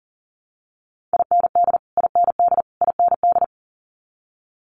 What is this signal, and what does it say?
Morse code, heard as a keyed tone, about 40 wpm, 700 Hz. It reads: IND IND IND
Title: Multi-letter Phonemes Repeated 3x 40wpm